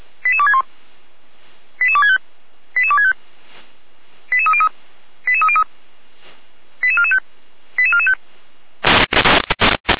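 Five-tone selective call (ZVEI), heard as quick runs of five tones, tones of about 70 ms. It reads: 88261 88265 88265 88303 88445 88445
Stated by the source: ZVEI five-tone selective call 88255 received with an SDR.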